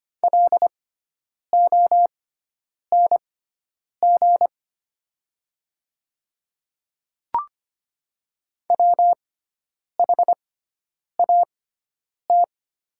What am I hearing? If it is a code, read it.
Morse code, heard as a keyed tone, about 25 wpm, spaced as for 9 wpm, 700 Hz. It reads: LONG WHAT
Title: 6x Character Spacing - Top 100 Words - 25wpm